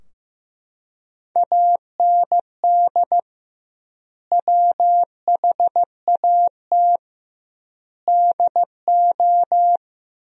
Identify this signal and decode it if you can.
Morse code, heard as a keyed tone, about 15 wpm, 700 Hz. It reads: AND WHAT DO
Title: 2x Word-Spacing - Sentences from Top 100 Words - 15wpm